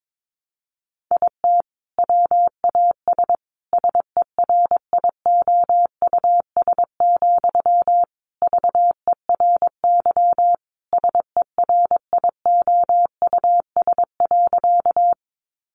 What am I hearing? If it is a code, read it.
Morse code, heard as a keyed tone, about 22 wpm, 700 Hz. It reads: IT WAS SERIOUS, VERY SERIOUS.
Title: Sentences from Top 1100 Words 22wpm